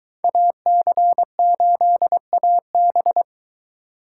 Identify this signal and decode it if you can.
Morse code, heard as a keyed tone, about 21 wpm, 700 Hz. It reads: AC8AB